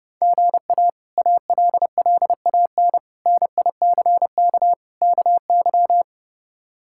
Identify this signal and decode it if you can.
Morse code, heard as a keyed tone, about 30 wpm, 700 Hz. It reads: GA ALLAN NICK KY
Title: K1USN SST QSO Practice - 30wpm